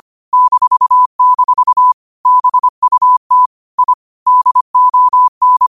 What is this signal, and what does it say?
Morse code, heard as a keyed tone, about 25 wpm, 1000 Hz. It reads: -- BUT I DON
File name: qotdfiles / 25wpm